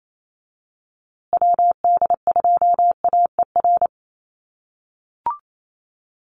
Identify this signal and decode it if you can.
Morse code, heard as a keyed tone, about 28 wpm, 700 Hz. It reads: WD2AER
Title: USA Call Signs Repeated 3x 28wpm